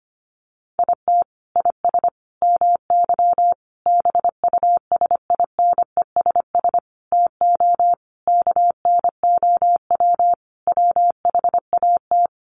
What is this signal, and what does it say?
Morse code, heard as a keyed tone, about 25 wpm, 700 Hz. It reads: IT IS MY BUSINESS TO KNOW WHAT
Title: Sentences from Top 600 Words 25wpm